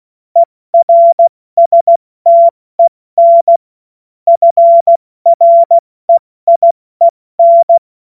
Morse code, heard as a keyed tone, about 16 wpm, 680 Hz. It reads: ERSTEN FREIEN
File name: Morsekurs / Witze / Witz 31.Oktober 2023 70 F80